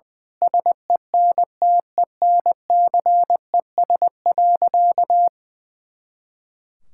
Morse code, heard as a keyed tone, about 20 wpm, 700 Hz. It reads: SENTENCES.
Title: Pangram Challenge v2 - 20wpm